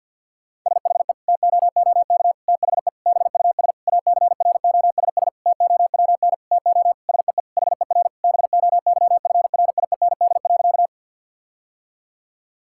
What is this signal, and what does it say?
Morse code, heard as a keyed tone, about 50 wpm, 700 Hz. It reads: SHE TOOK THE BUS ACROSS TOWN TO SEE HER BOYFRIEND.
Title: Sentences from Top 1000 Words - Review Part 6 - 50wpm